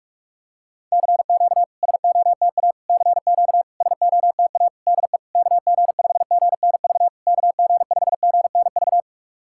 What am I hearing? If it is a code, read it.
Morse code, heard as a keyed tone, about 46 wpm, 660 Hz. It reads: CQ SOTA CQ SOTA DE KG5GNV KG5GNV
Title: SOTA QSO Practice 45wpm